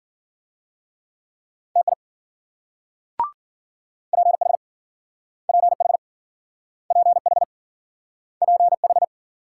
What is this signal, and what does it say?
Morse code, heard as a keyed tone, about 52 wpm, 700 Hz. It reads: TI PH PH PH PH